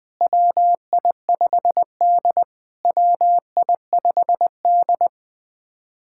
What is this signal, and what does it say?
Morse code, heard as a keyed tone, about 20 wpm, 700 Hz. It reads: WI5D WI5D
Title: POTA QSO Practice - 20wpm